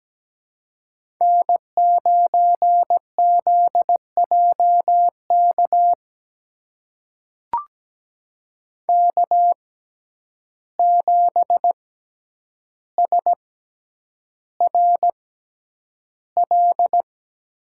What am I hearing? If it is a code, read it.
Morse code, heard as a keyed tone, about 17 wpm, spaced as for 9 wpm, 700 Hz. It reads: N9ZJK K7SRL
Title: USA Call Signs - 5 Characters - 6x Character Spacing - 17wpm